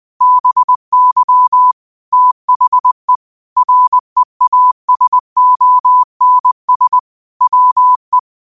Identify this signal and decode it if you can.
Morse code, heard as a keyed tone, about 20 wpm, 1000 Hz. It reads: BY THE REASONS WE